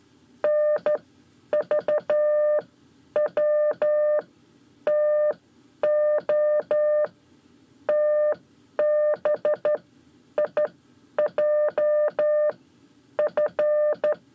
Morse code, heard as a keyed tone, about 12 wpm, spaced as for 9 wpm, 600 Hz. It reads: NVWTO TBIJF
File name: 5 Lezione CW 1